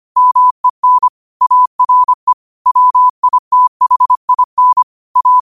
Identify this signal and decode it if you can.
Morse code, heard as a keyed tone, about 25 wpm, 1000 Hz. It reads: MEN ARE WITHIN A